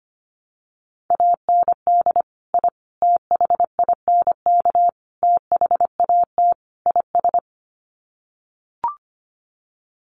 Morse code, heard as a keyed tone, about 25 wpm, 700 Hz. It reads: AND I THINK THAT IS
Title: Sets of 5 Words Encore2 25wpm